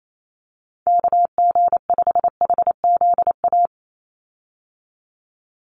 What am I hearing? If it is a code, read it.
Morse code, heard as a keyed tone, about 28 wpm, 700 Hz. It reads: KG5HZA
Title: USA Call Signs 28wpm